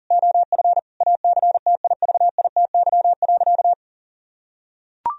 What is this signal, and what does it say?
Morse code, heard as a keyed tone, about 40 wpm, 700 Hz. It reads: OF ACTIVITY.
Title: Sentences from Top 1000 Words - Review Part 4 - 40wpm